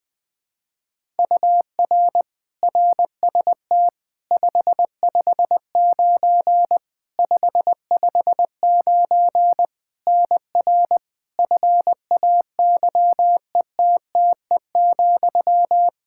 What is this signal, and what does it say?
Morse code, heard as a keyed tone, about 20 wpm, 700 Hz. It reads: UR RST 559 559 NR FAYETTE,
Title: Practice QSO - Exchange 1 - Beginning - 20wpm